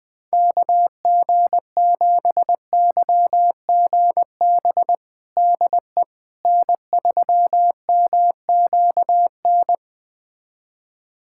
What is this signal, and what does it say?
Morse code, heard as a keyed tone, about 20 wpm, 690 Hz. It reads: KG7YGB DE N3MQN
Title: USA Callsign DE Callsign 20wpm